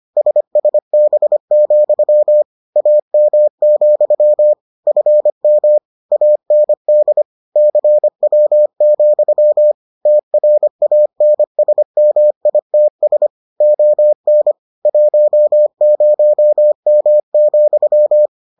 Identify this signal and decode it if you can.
Morse code, heard as a keyed tone, about 25 wpm, 600 Hz. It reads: SSB, AM, FM AND CW, TRANSMITS ON 10M,